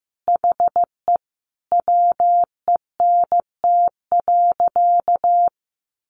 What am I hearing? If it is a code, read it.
Morse code, heard as a keyed tone, about 15 wpm, 700 Hz. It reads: HE WENT.
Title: Sentences from Top 600 Words 15wpm